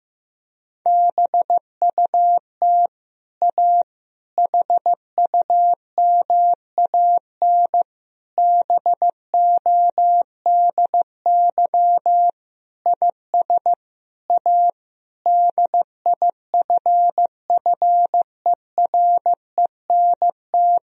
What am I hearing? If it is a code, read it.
Morse code, heard as a keyed tone, about 15 wpm, 700 Hz. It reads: BUT A HUMAN BODY IS A DIFFERENT
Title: Sentences from Top 1000 Words - Review Part 3 - 15wpm